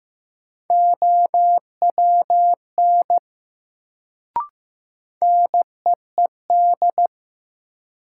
Morse code, heard as a keyed tone, about 15 wpm, 700 Hz. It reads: OWN NEED
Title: Sets of 2 Words - New Word Drill - 15wpm